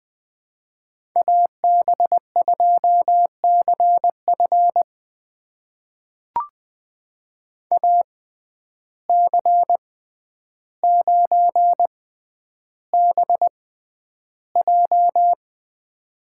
Morse code, heard as a keyed tone, about 20 wpm, spaced as for 10 wpm, 700 Hz. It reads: AB2CF AC9BJ
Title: USA Call Signs - 5 Characters - 6x Character Spacing - 20wpm